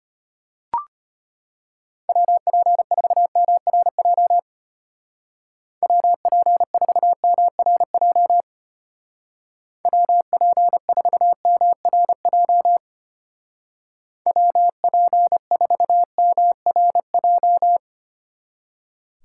Morse code, heard as a keyed tone, about 31 wpm, 700 Hz. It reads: WP4MRJ WP4MRJ WP4MRJ WP4MRJ
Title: Speed Racing - USA 2x3 Call Signs 25wpm